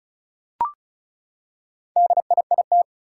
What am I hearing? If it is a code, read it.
Morse code, heard as a keyed tone, about 36 wpm, 700 Hz. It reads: DIIT
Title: Four Challenging Letter-Number Combinations 35wpm